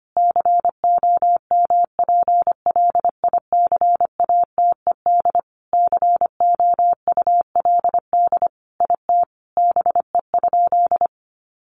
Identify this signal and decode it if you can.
Morse code, heard as a keyed tone, about 25 wpm, 700 Hz. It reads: COMPLICATED COULD IT BE?
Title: Sentences from Top 1200 Words 25wpm